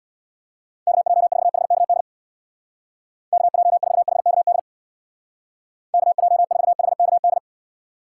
Morse code, heard as a keyed tone, about 64 wpm, 700 Hz. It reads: KJ4HKD KJ4HKD KJ4HKD